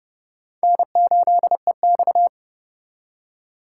Morse code, heard as a keyed tone, about 28 wpm, 700 Hz. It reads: N8EX